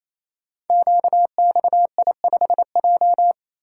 Morse code, heard as a keyed tone, about 29 wpm, 700 Hz. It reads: QXI5J
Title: Code Groups - 5 Letter:Numbers - 28wpm